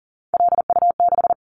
Morse code, heard as a keyed tone, about 40 wpm, 700 Hz. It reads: LU6
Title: Callsign Prefixes for Top 100 DXCC - 40wpm